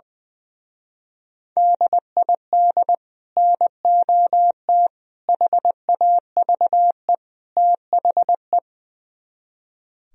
Morse code, heard as a keyed tone, about 21 wpm, 700 Hz. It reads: DID NOT HAVE THE
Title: Speed-Racing - Sets of 4 Words 20wpm